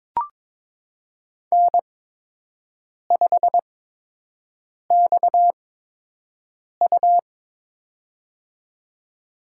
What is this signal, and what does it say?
Morse code, heard as a keyed tone, about 22 wpm, spaced as for 6 wpm, 700 Hz. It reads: N5XU
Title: USA Call Signs - 4 Characters - 8x Character Spacing - 22wpm